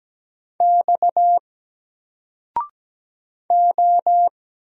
Morse code, heard as a keyed tone, about 17 wpm, 700 Hz. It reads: X O
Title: Single Letter-Number Alt 17wpm